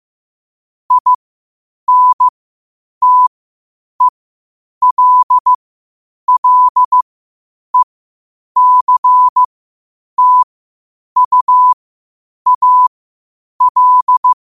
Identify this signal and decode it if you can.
Morse code, heard as a keyed tone, about 15 wpm, spaced as for 8 wpm, 1000 Hz. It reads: INTELLECTUAL
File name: qotdfiles / 8wpm